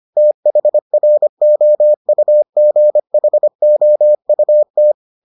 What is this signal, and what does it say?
Morse code, heard as a keyed tone, about 25 wpm, 600 Hz. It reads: THROUGHOUT